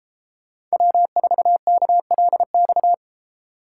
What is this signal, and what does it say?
Morse code, heard as a keyed tone, about 31 wpm, 700 Hz. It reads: W4KLX